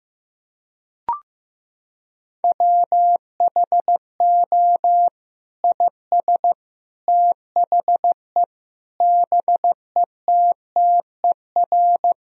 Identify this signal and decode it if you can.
Morse code, heard as a keyed tone, about 15 wpm, 700 Hz. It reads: WHO IS THE BETTER
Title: Sentences from Top 1000 Words - Review Part 4 - 15wpm